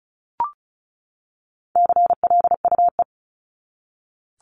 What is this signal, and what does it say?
Morse code, heard as a keyed tone, about 34 wpm, 700 Hz. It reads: CLUE
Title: Top 1751-2000 Words 35wpm